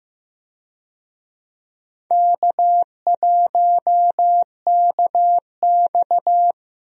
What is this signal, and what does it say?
Morse code, heard as a keyed tone, about 15 wpm, 700 Hz. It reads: K1KX
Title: USA Call Signs - 4 Characters - 15wpm